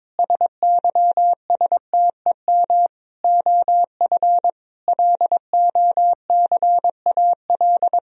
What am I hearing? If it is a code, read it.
Morse code, heard as a keyed tone, about 22 wpm, 700 Hz. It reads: SYSTEM OF LOCAL
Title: Sentences from Top 1000 Words - Review Part 3 - 22wpm